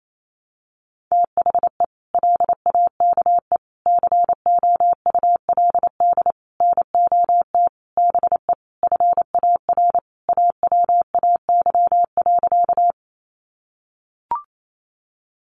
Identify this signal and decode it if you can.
Morse code, heard as a keyed tone, about 28 wpm, 700 Hz. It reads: THE LAKE COULD NOT BE FAR AWAY.
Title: Sentences from Top 1100 Words 28wpm